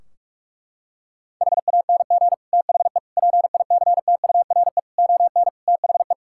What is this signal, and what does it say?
Morse code, heard as a keyed tone, about 46 wpm, 700 Hz. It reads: HANG THE PICTURE ON THE
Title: Sentences from Top 1600 Words 45wpm